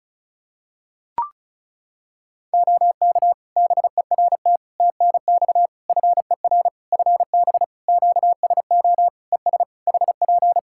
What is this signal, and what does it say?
Morse code, heard as a keyed tone, about 36 wpm, 700 Hz. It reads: OK BERT TNX FER FB QSO ES HP
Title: Practice QSO - Exchange 4 - Ending - 35wpm